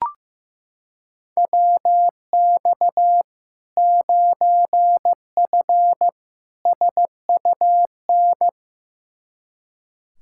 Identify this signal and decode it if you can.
Morse code, heard as a keyed tone, about 15 wpm, 700 Hz. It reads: WX 9F SUN
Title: Practice QSO - Exchange 2 - Info - 15wpm